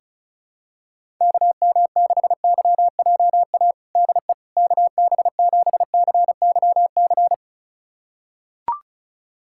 Morse code, heard as a keyed tone, about 36 wpm, 700 Hz. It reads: KM6YJA DE KB7CYC
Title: USA Callsign DE Callsign - 6 Characters - 35wpm